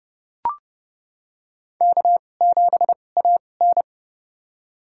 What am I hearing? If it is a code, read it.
Morse code, heard as a keyed tone, about 29 wpm, spaced as for 22 wpm, 700 Hz. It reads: K7AN
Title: USA Call Signs - 2x Character Spacing - 30wpm